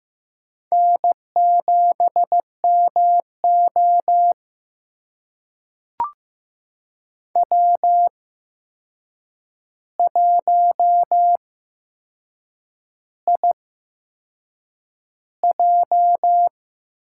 Morse code, heard as a keyed tone, about 15 wpm, spaced as for 6 wpm, 700 Hz. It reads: N7MO W1IJ
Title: USA Call Signs - 4 Characters - 8x Character Spacing - 15wpm